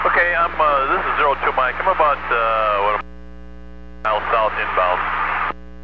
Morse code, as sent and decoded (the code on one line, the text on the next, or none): none